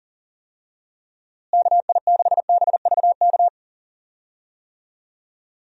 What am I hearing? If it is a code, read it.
Morse code, heard as a keyed tone, about 40 wpm, 700 Hz. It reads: KI6BVK
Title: USA Call Signs - 6 Characters - Repeated 3x - 40wpm